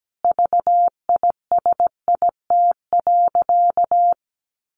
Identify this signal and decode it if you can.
Morse code, heard as a keyed tone, about 17 wpm, 700 Hz. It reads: VISIT.
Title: Sentences from Top 1200 Words 17wpm